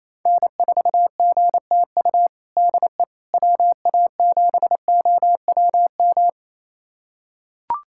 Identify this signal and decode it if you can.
Morse code, heard as a keyed tone, about 28 wpm, 700 Hz. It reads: N4GTU DE WA7OWM
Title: USA Callsign DE Callsign 28wpm